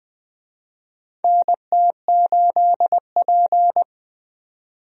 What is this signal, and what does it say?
Morse code, heard as a keyed tone, about 21 wpm, 700 Hz. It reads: NT8P
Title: USA Call Signs - 4 Characters - 20wpm